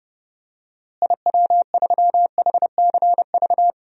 Morse code, heard as a keyed tone, about 30 wpm, 700 Hz. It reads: IW3HCV
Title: DXCC-100 Call Signs - 6 Characters - Repeated 3x - 30wpm